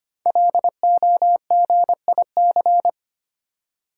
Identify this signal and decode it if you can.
Morse code, heard as a keyed tone, about 25 wpm, 700 Hz. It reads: LOGIC
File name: 5 Letter Words - First 3 Letters Given - 25wpm